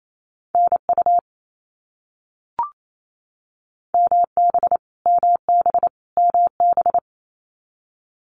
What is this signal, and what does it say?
Morse code, heard as a keyed tone, about 28 wpm, 700 Hz. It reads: NU MB MB MB
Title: Canadian Provinces and Territories - Repeated 3x - 28wpm